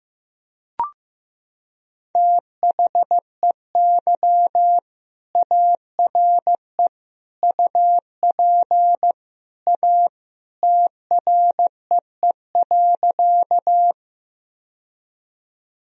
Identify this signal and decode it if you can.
Morse code, heard as a keyed tone, about 15 wpm, 700 Hz. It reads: THEY ARE UP A TREE.
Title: Sentences from Top 1000 Words - Review Part 1 - 15wpm